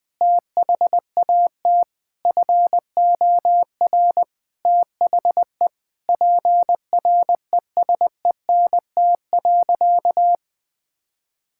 Morse code, heard as a keyed tone, about 20 wpm, 700 Hz. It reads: THAT FOR THE PRESENT.
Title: Sentences from Top 1000 Words - Review Part 5 - 20wpm